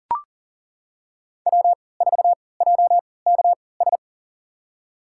Morse code, heard as a keyed tone, about 40 wpm, spaced as for 23 wpm, 700 Hz. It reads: W4JKS
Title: USA Call Signs - 3x Character Spacing - 40wpm